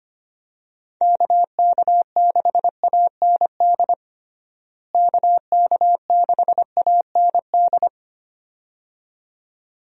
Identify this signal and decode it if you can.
Morse code, heard as a keyed tone, about 25 wpm, 700 Hz. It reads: KK6AND KK6AND
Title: USA Call Signs - 6 Characters - Repeated 2x - 25wpm